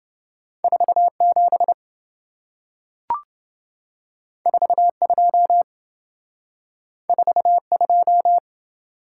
Morse code, heard as a keyed tone, about 27 wpm, 700 Hz. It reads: 47 42 42